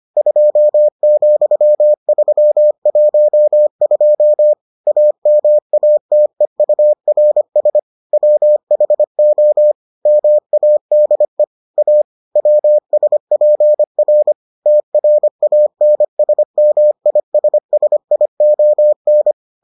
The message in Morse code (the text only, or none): 2,312 AMATEURS WHO MADE A WSPR TRANSMISSION